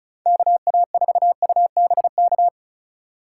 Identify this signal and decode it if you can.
Morse code, heard as a keyed tone, about 45 wpm, 700 Hz. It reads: KA4UBK